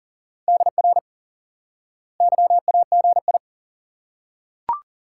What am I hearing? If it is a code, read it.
Morse code, heard as a keyed tone, about 38 wpm, 700 Hz. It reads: DR YAGI
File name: IQE Course 3 - Lesson 89 - YAGI - Two Elements - 40wpm